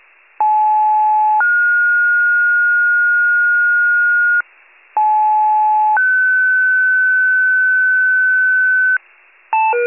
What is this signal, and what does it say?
Two-tone page, heard as a one-second tone, then a three-second tone, about 860 then 1430 Hz, about 860 then 1500 Hz. It is heard from 0.4 to 9.0 s.